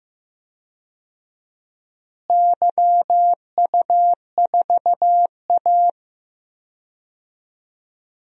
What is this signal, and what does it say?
Morse code, heard as a keyed tone, about 15 wpm, 700 Hz. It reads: YU4A
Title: DXCC-100 Call Signs - 4 Characters - Repeated 3x - 15wpm